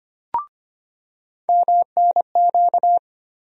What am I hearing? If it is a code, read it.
Morse code, heard as a keyed tone, about 25 wpm, 700 Hz. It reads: MNQ
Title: USA Call Sign Postfixes - 3 Letters 25wpm